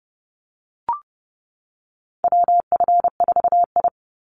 Morse code, heard as a keyed tone, about 30 wpm, 700 Hz. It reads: WF4I
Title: NA SOTA Chaser Callsigns 30wpm